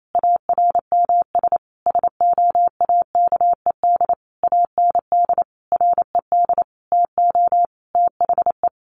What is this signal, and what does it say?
Morse code, heard as a keyed tone, about 28 wpm, 700 Hz. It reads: ARMS SOAKED AND RED TO THE
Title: A Sound Of Thunder 28wpm